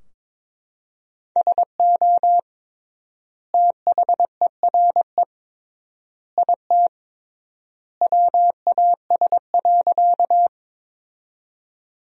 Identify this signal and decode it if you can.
Morse code, heard as a keyed tone, about 22 wpm, 700 Hz. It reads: SO THERE IT WAS.
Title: 3x Word-Spacing - Sentences from Top 100 Words - 22wpm